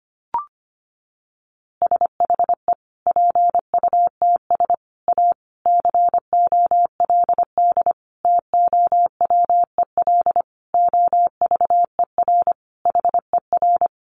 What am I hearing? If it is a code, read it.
Morse code, heard as a keyed tone, about 25 wpm, 700 Hz. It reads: SHE PUTS A COLD TOWEL OVER HER
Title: Easy Phrases 25wpm